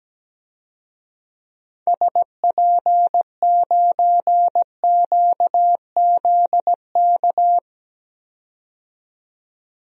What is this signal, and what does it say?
Morse code, heard as a keyed tone, about 17 wpm, 700 Hz. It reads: SP9QZK
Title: DXCC-100 Call Signs - 6 Characters - 17wpm